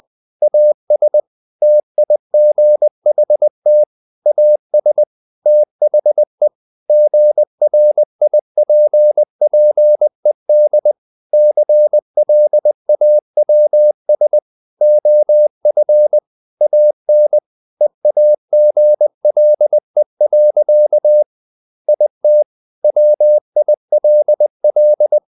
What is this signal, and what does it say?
Morse code, heard as a keyed tone, about 20 wpm, 600 Hz. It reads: AS TIGHT AS THE GRIPPED CLAWS OF AN EAGLE. IT WILL
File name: the old man and the sea / the old man and the sea part 1 20wpm